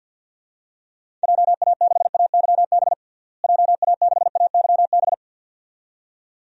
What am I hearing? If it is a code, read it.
Morse code, heard as a keyed tone, about 53 wpm, 700 Hz. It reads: JA6AYB JA6AYB
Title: DXCC-100 Call Signs - 6 Characters - Repeated 2x - 50wpm